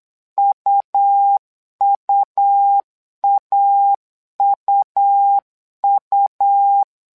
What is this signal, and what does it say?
Morse code, heard as a keyed tone, about 8 wpm, 800 Hz. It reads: UUAUU
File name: LEZIONI / 2 UHDN / 2 U 8wpm 2028